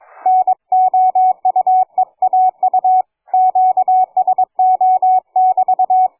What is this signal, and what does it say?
Morse code, heard as a keyed tone, about 22 wpm, 740 Hz, with radio noise. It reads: NOUEAU QSO<BT>
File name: cw forum / fg5fr 160m 02h30